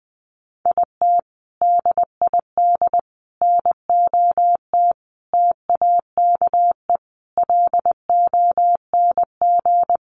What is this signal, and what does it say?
Morse code, heard as a keyed tone, about 19 wpm, 700 Hz. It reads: IT DID NOT TAKE LONG
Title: Sets of 5 Words Encore2 20wpm